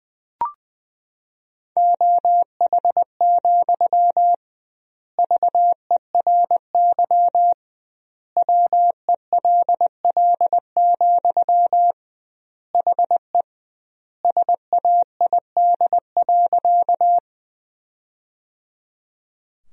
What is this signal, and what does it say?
Morse code, heard as a keyed tone, about 20 wpm, 700 Hz. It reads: OH, VERY WELL, HE SAID.
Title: 2x Word-Spacing - Sentences from Top 500 Words - 20wpm